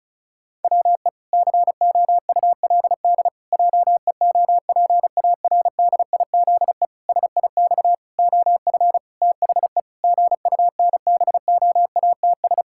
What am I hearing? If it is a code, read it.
Morse code, heard as a keyed tone, about 34 wpm, 700 Hz. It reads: WE COULD JEOPARDIZE SIX OF THE GUNBOATS
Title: Pangram Challenge v2 - 35wpm